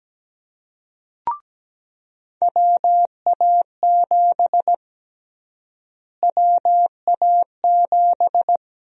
Morse code, heard as a keyed tone, about 17 wpm, 700 Hz. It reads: WA7 WA7
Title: USA Call Sign Prefixes - Repeated 2x - 17wpm